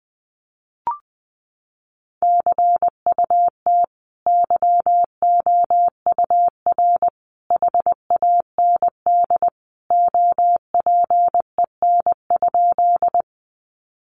Morse code, heard as a keyed tone, about 20 wpm, 700 Hz. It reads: CUT YOUR HAND OPEN?
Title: Sentences from Top 300 Words 20wpm